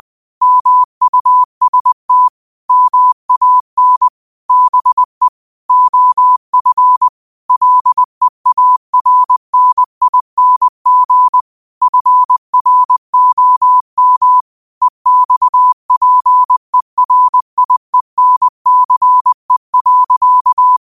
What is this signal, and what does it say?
Morse code, heard as a keyed tone, about 20 wpm, 1000 Hz. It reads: MUST MAN BE OF LEARNING FROM EXPERIENCE.
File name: qotdfiles / 20wpm